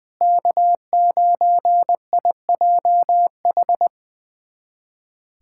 Morse code, heard as a keyed tone, about 21 wpm, 700 Hz. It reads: K9IJH
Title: USA Callsign DE Callsign - 5 Characters - 20wpm